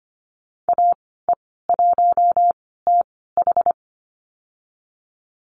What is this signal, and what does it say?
Morse code, heard as a keyed tone, about 25 wpm, spaced as for 16 wpm, 700 Hz. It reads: AE1TH
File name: Lesson 15g - Number 5 - Call Signs 25wpm